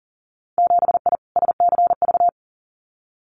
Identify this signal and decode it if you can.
Morse code, heard as a keyed tone, about 40 wpm, 700 Hz. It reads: 7I SCV
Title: Winter Field Day Exchanges Encore 40wpm